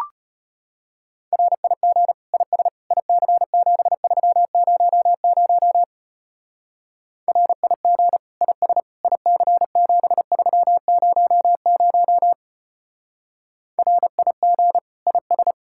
Morse code, heard as a keyed tone, about 31 wpm, 700 Hz. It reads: RIG IS IC7300 RIG IS IC7300 RIG IS